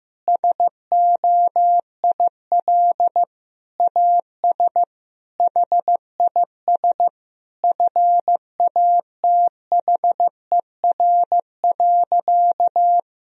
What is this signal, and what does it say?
Morse code, heard as a keyed tone, about 15 wpm, 700 Hz. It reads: SOIL AS HIS FATHER.
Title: Sentences from Top 1000 Words - Review Part 4 - 15wpm